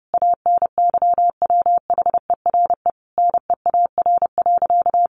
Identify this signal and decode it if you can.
Morse code, heard as a keyed tone, about 30 wpm, 700 Hz. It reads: ANYWHERE NEAR.
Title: Sentences from Top 1500 Words 30wpm